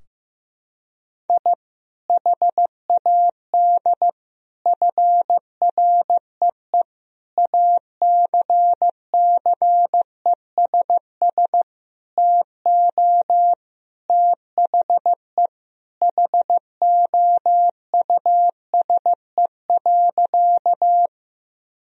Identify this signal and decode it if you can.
Morse code, heard as a keyed tone, about 15 wpm, 700 Hz. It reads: I HAD FREE ACCESS TO THE HOUSE.
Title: Sentences from Top 1000 Words - Review Part 4 - 15wpm